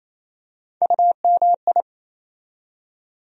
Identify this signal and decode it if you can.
Morse code, heard as a keyed tone, about 28 wpm, 700 Hz. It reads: UMI
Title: USA Call Sign Postfixes - 3 Letters 28wpm